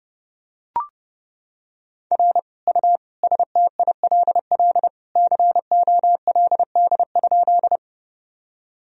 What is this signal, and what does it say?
Morse code, heard as a keyed tone, about 30 wpm, 700 Hz. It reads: R U STILL COLD?
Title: Sentences from Top 400 Words and Abbreviations 30wpm